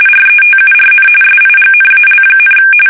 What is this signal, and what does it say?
RTTY, about 45.45 baud. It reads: RETURN SIGNAL,
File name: fsk850